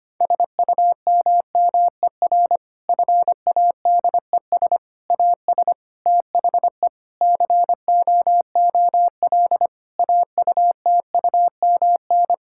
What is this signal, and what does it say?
Morse code, heard as a keyed tone, about 25 wpm, 700 Hz. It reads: SUMMER FADES AS THE COOL AUTUMN
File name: practice / 25wpm / Difficult Sentences with Hints - 25wpm